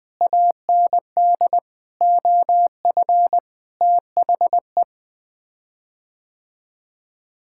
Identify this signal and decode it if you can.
Morse code, heard as a keyed tone, about 20 wpm, 700 Hz. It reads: AND OF THE
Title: Sets of 9 Words - 20wpm